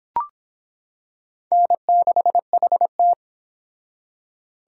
Morse code, heard as a keyed tone, about 25 wpm, 700 Hz. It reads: N6HT